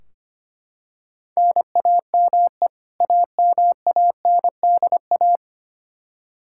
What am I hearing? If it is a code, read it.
Morse code, heard as a keyed tone, about 25 wpm, 700 Hz. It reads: NAME AMANDA
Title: Practice QSO - Exchange 1 - Beginning - 25wpm